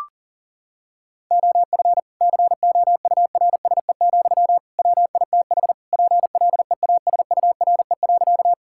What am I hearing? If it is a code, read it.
Morse code, heard as a keyed tone, about 40 wpm, 700 Hz. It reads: OF COURSE, WITH PLEASURE.
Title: Sentences from Top 1600 Words 40wpm